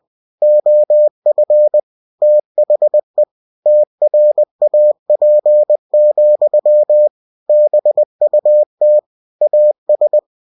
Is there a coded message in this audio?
Yes: Morse code from 0.4 to 10.2 s, heard as a keyed tone, about 20 wpm, 600 Hz. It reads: OF THE TRAP, BUT AS